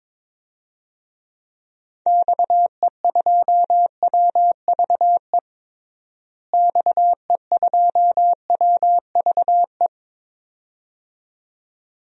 Morse code, heard as a keyed tone, about 22 wpm, 700 Hz. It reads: XE2WVE XE2WVE
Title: DXCC-100 Call Signs - 6 Characters - Repeated 2x - 22wpm